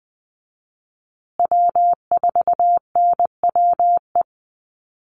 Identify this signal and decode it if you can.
Morse code, heard as a keyed tone, about 20 wpm, 700 Hz. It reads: W4NWE
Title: USA Call Signs Repeated 2x 20wpm